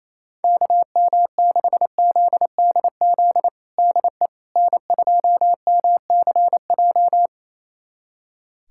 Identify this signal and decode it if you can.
Morse code, heard as a keyed tone, about 28 wpm, 700 Hz. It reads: KM6ZDZ DE N2MCJ
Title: USA Callsign DE Callsign 28wpm